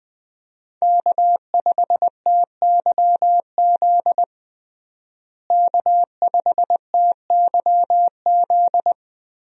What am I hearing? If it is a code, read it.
Morse code, heard as a keyed tone, about 21 wpm, 700 Hz. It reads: K5TYZ K5TYZ
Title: USA Call Signs - 5 Characters - Repeated 3x - 20wpm